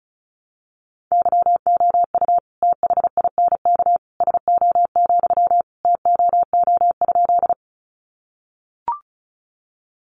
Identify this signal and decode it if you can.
Morse code, heard as a keyed tone, about 34 wpm, 700 Hz. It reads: YOU THINK SO, TOO?
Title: Sentences from Top 200 Words 35wpm